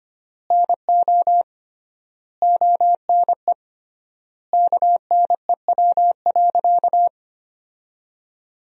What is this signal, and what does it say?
Morse code, heard as a keyed tone, about 25 wpm, 700 Hz. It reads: NO ONE KNEW.
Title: 3x Word-Spacing - Sentences from Top 400 Words - 25wpm